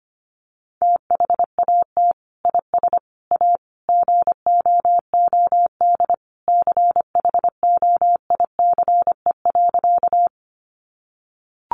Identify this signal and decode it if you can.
Morse code, heard as a keyed tone, about 25 wpm, 700 Hz. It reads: THAT IS A GOOD CHOICE.
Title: Sentences from Top 900 Words 25wpm